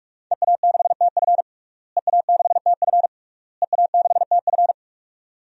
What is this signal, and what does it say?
Morse code, heard as a keyed tone, about 42 wpm, 700 Hz. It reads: EA6TF EA6TF EA6TF
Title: DXCC-100 Call Signs - 5 Characters - Repeated 3x - 45wpm